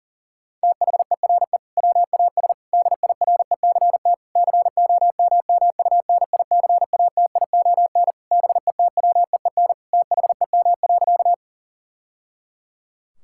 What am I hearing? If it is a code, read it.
Morse code, heard as a keyed tone, about 40 wpm, 700 Hz. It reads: THERE WAS DIRECT COMMUNICATION BETWEEN THEM.
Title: Sentences from Top 1000 Words - Review Part 6 - 40wpm